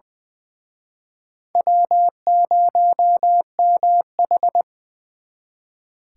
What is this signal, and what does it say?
Morse code, heard as a keyed tone, about 23 wpm, 700 Hz. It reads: W0MH
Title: Speed Racing - USA 1x2 Call Signs 20wpm